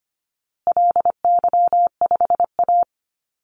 Morse code, heard as a keyed tone, about 25 wpm, 700 Hz. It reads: LY5A
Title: Call Signs Repeated 2x - 25wpm